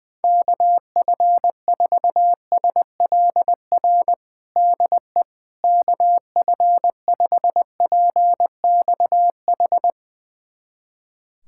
Morse code, heard as a keyed tone, about 21 wpm, 700 Hz. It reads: KF4SLR DE KF5PXH
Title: USA Callsign DE Callsign - 6 Characters - 20wpm